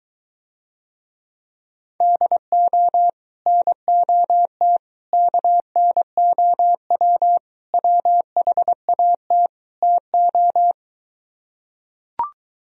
Morse code, heard as a keyed tone, about 22 wpm, 700 Hz. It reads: DO NOT KNOW WHAT TO